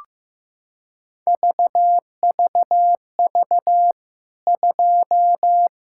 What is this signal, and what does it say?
Morse code, heard as a keyed tone, about 15 wpm, 700 Hz. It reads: VVV 2
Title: K1USN SST QSO Practice 15wpm